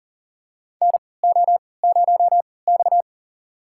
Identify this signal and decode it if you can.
Morse code, heard as a keyed tone, about 40 wpm, spaced as for 23 wpm, 700 Hz. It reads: NO0X
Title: USA Call Signs - 4 Characters - 3x Character Spacing - 40wpm